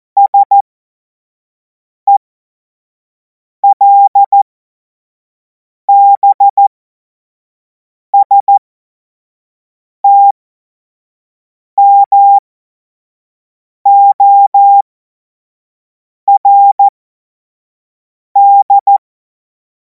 Morse code, heard as a keyed tone, about 14 wpm, spaced as for 5 wpm, 800 Hz. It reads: SELBSTMORD
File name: Morsekurs / Witze / Witz 24. März 2026 25 F70